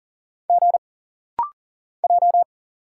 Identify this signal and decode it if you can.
Morse code, heard as a keyed tone, about 40 wpm, 700 Hz. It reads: G J
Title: Single Letters - Rapid-Fire 40wpm